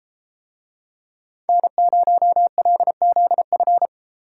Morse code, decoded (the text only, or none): N0LZF